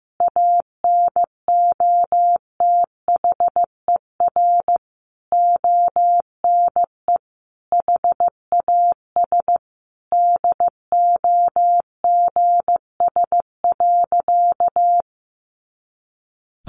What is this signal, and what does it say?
Morse code, heard as a keyed tone, about 15 wpm, 700 Hz. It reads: ANOTHER ONE HAS DOGS.
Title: Additional Easy Phrases 15wpm